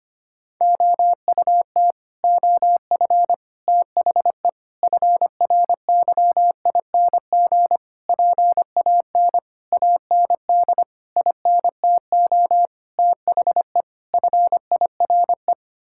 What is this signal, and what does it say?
Morse code, heard as a keyed tone, about 25 wpm, 700 Hz. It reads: OUT OF THE FRYING PAN AND INTO THE FIRE
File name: Sets of 9 Words - 25wpm